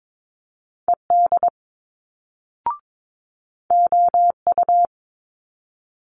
Morse code, heard as a keyed tone, about 22 wpm, 700 Hz. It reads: ED OU
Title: Multi-letter Phonemes Quick Fire 22wpm